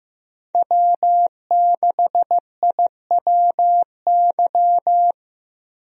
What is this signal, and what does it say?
Morse code, heard as a keyed tone, about 15 wpm, 700 Hz. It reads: W6IWY
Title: USA Call Signs - 5 Characters - Repeated 2x - 15wpm